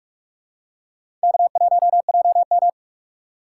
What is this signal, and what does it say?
Morse code, heard as a keyed tone, about 46 wpm, 700 Hz. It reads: K1JM
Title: USA Call Signs - 4 Characters - 3x Character Spacing - 45wpm